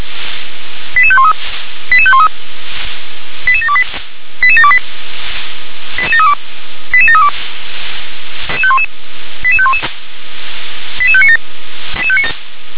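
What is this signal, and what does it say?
Five-tone selective call (ZVEI), heard as quick runs of five tones, tones of about 70 ms. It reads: 88513 88513 88728 88728 88732 88732 88511 88597
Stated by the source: ZVEI five-tone call received on SDR.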